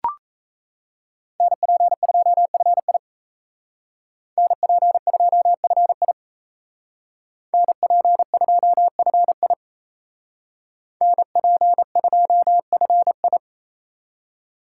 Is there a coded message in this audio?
Yes: Morse code from 1.4 to 13.4 s, heard as a keyed tone, about 36 wpm, 700 Hz. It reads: NP2FI NP2FI NP2FI NP2FI